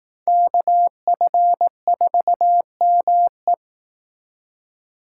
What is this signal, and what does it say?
Morse code, heard as a keyed tone, about 18 wpm, 700 Hz. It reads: KF4ME
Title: Speed Racing - USA 2x2 Call Signs 15wpm